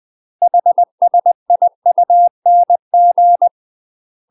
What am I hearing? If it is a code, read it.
Morse code, heard as a keyed tone, about 20 wpm, 700 Hz. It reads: HSIUNG
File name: UK MASTER QSOs TRAINING / 20WPM / UK MASTER QSOs 0072